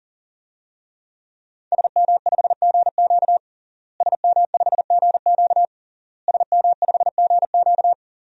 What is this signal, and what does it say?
Morse code, heard as a keyed tone, about 40 wpm, 700 Hz. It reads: SM5GQ SM5GQ SM5GQ
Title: DXCC-100 Call Signs - 5 Characters - Repeated 3x - 40wpm